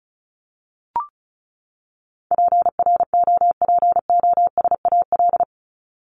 Morse code, heard as a keyed tone, about 34 wpm, 700 Hz. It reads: PROPOSAL
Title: Top 1751-2000 Words 35wpm